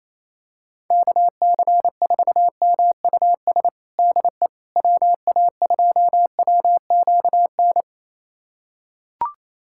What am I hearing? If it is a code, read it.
Morse code, heard as a keyed tone, about 27 wpm, 700 Hz. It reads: KC4MUS DE WA2WQN
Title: USA Callsign DE Callsign 28wpm